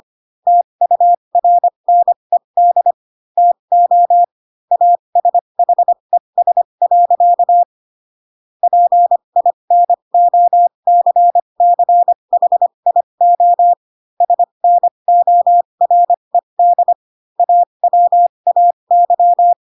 Morse code, heard as a keyed tone, about 25 wpm, spaced as for 22 wpm, 700 Hz. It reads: TURNED TO ASHES. PINOCCHIO SNORED AWAY
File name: Pinocchio / Pinocchio ch6 18 25 ik1hge